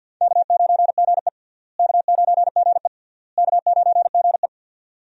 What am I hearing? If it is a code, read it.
Morse code, heard as a keyed tone, about 46 wpm, 700 Hz. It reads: K9GE K9GE K9GE
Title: USA Call Signs - 4 Characters - Repeated 3x - 50wpm